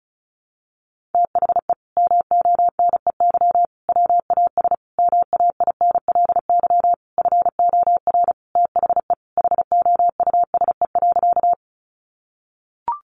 Morse code, heard as a keyed tone, about 34 wpm, 700 Hz. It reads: THE MONEY WAS MAINLY FOR THE HOUSE.
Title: Sentences from Top 1400 Words 35wpm